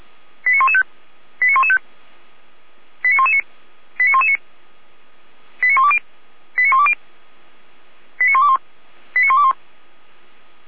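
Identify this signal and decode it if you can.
Five-tone selective call (ZVEI), heard as quick runs of five tones, tones of about 70 ms. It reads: 89115 89115 89119 89119 89120 89120 89121 89121
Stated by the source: ZVEI five-tone selective call received on SDR.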